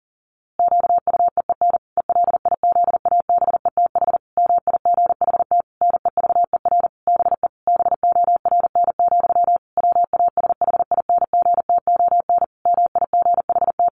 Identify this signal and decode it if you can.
Morse code, heard as a keyed tone, about 40 wpm, 700 Hz. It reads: QUEEN ELIZABETH MIGHT NEVER BE BORN, WASHINGTON MIGHT
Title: A Sound Of Thunder 40wpm